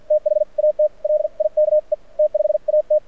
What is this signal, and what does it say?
Morse code, heard as a keyed tone, about 53 wpm, spaced as for 43 wpm, 590 Hz. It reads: THAT LIKE T5AT
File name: archived / CRSnet